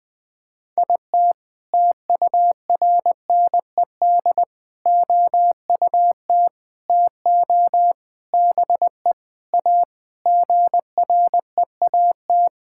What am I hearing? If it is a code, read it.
Morse code, heard as a keyed tone, about 21 wpm, 700 Hz. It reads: IT TURNED OUT TO BE A GREAT
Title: Sentences with 4-Word Phrases - 20wpm